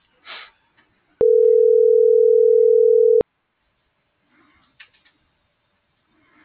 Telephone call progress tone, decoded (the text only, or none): ringback tone